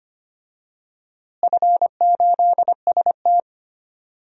Morse code, heard as a keyed tone, about 25 wpm, 700 Hz. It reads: F8ST
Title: DXCC-100 Call Signs - 4 Characters - Repeated 3x - 25wpm